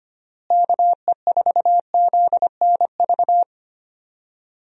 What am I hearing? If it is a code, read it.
Morse code, heard as a keyed tone, about 25 wpm, 700 Hz. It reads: KE4ZNV
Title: USA Call Signs - 6 Characters - Repeated 3x - 25wpm